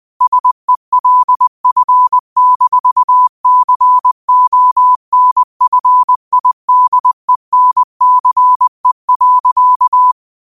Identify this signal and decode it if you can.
Morse code, heard as a keyed tone, about 20 wpm, 1000 Hz. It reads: SELF-CONFIDENCE.
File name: qotdfiles / 20wpm